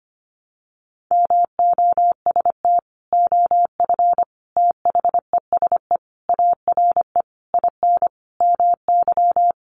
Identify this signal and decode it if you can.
Morse code, heard as a keyed tone, about 25 wpm, 700 Hz. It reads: MOST OF THESE ARE IN MY
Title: Sentences from Top 1200 Words 25wpm